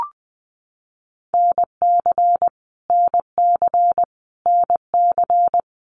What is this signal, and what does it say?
Morse code, heard as a keyed tone, about 20 wpm, 700 Hz. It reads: NC NC NC
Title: US States Repeated 3x 20wpm